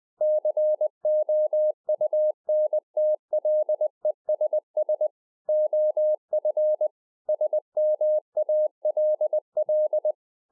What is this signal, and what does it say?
Morse code, heard as a keyed tone, about 20 wpm, 610 Hz. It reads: COUNTLESS OF SMALL